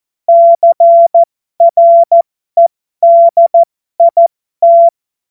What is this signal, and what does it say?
Morse code, heard as a keyed tone, about 14 wpm, spaced as for 12 wpm, 680 Hz. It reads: CREDIT
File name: Morsekurs / Witze